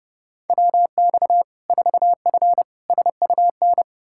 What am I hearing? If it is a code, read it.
Morse code, heard as a keyed tone, about 30 wpm, 700 Hz. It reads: WX 4F SUN
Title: Practice QSO - Exchange 2 - Info - 30wpm